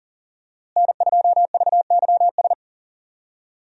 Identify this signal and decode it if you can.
Morse code, heard as a keyed tone, about 40 wpm, 700 Hz. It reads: N2VYS
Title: USA Call Signs - 5 Characters - Repeated 2x - 40wpm